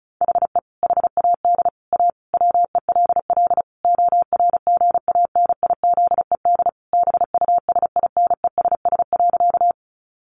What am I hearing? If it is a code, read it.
Morse code, heard as a keyed tone, about 34 wpm, 700 Hz. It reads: HE HAD A WELL ORGANIZED BUSINESS.
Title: Sentences from Top 1300 Words 35wpm